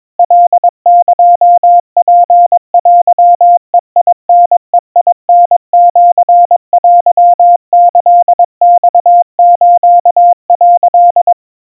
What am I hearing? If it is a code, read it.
Morse code (Japanese Wabun code), heard as a keyed tone, about 22 wpm, 700 Hz. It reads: カエツテベタベタシテキマス」
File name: 80 cpm 700 Hz Crispy